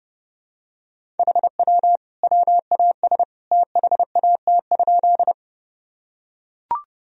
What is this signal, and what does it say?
Morse code, heard as a keyed tone, about 30 wpm, 700 Hz. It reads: HW WAS THAT?
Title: Sentences from Top 100 Words and Abbreviations 30wpm